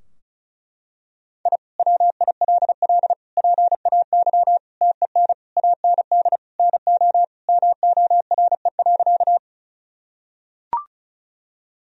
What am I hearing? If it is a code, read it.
Morse code, heard as a keyed tone, about 34 wpm, 700 Hz. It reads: I WILL PAY TEN AND NO MORE.
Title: Sentences from Top 1000 Words - Review Part 2 - 35wpm